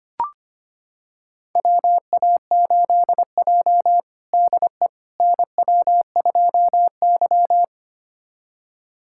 Morse code, heard as a keyed tone, about 25 wpm, 700 Hz. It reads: WA8J DE NW2Y
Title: USA Callsign DE Callsign - 4 Characters - 25wpm